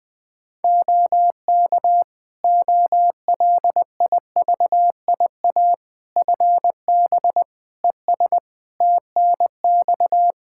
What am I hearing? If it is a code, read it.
Morse code, heard as a keyed tone, about 20 wpm, 700 Hz. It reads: OK OLIVIA FB ES TNX
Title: Practice QSO - Exchange 2 - Info - 20wpm